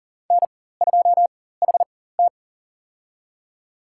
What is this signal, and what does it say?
Morse code, heard as a keyed tone, about 40 wpm, spaced as for 19 wpm, 700 Hz. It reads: N2HT
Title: USA Call Signs - 4 Characters - 4x Character Spacing - 40wpm